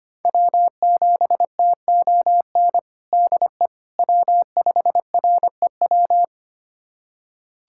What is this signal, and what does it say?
Morse code, heard as a keyed tone, about 25 wpm, 700 Hz. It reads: W7TON DE W5REW
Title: USA Callsign DE Callsign - 5 Characters - 25wpm